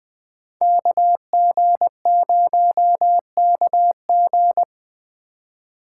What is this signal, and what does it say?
Morse code, heard as a keyed tone, about 20 wpm, 700 Hz. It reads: KG0KG
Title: Speed Racing - USA 2x2 Call Signs 17wpm